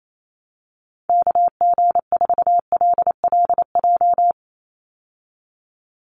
Morse code, heard as a keyed tone, about 28 wpm, 700 Hz. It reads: KG4LLJ
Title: USA Call Signs 28wpm